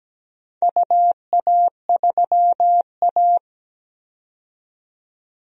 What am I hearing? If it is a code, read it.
Morse code, heard as a keyed tone, about 17 wpm, 700 Hz. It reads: UA3A
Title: DXCC-100 Call Signs - 4 Characters - 17wpm